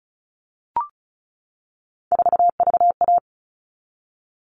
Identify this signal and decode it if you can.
Morse code, heard as a keyed tone, about 34 wpm, 700 Hz. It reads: 4VA
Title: All Cut Numbers - 3-Digit 35wpm